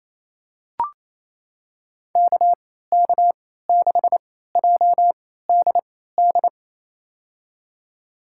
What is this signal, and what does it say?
Morse code, heard as a keyed tone, about 28 wpm, spaced as for 16 wpm, 700 Hz. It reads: KK6JDD
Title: USA Call Signs - 3x Character Spacing - 28wpm